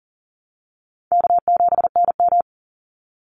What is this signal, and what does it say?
Morse code, heard as a keyed tone, about 40 wpm, 700 Hz. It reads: K7NM
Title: Call Signs Repeated 2x - 40wpm